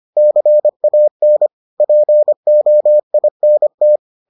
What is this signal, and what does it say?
Morse code, heard as a keyed tone, about 25 wpm, 600 Hz. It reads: CAN POINT